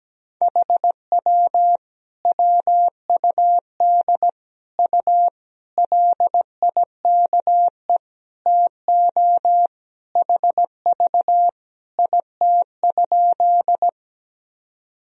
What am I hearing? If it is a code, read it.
Morse code, heard as a keyed tone, about 17 wpm, 700 Hz. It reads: HW WUD U LIKE TO HV IT?
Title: Sentences from Top 100 Words and Abbreviations 17wpm